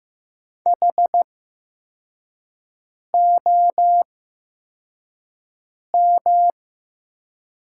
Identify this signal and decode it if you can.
Morse code, heard as a keyed tone, about 15 wpm, spaced as for 4 wpm, 700 Hz. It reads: HOM
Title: 8x Character Spacing - Top 200 Words - 15wpm